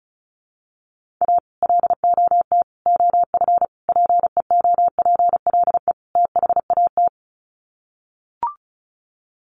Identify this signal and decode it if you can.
Morse code, heard as a keyed tone, about 34 wpm, 700 Hz. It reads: A LOT OF PEOPLE THAT
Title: Sets of 5 Words Encore2 35wpm